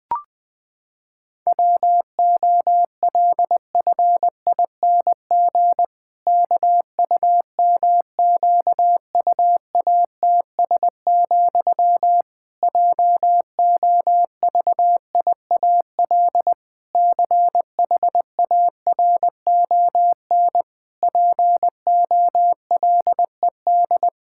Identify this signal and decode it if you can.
Morse code, heard as a keyed tone, about 21 wpm, 700 Hz. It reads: WOLFING KUMQUATS, JOVIAL CHARON POLED
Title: Pangram Challenge v2 - 20wpm